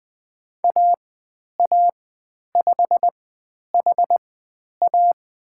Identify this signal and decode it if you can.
Morse code, heard as a keyed tone, about 20 wpm, spaced as for 10 wpm, 700 Hz. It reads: AA5HA
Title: 095 Lesson 16g - Letter c - Call Signs 20wpm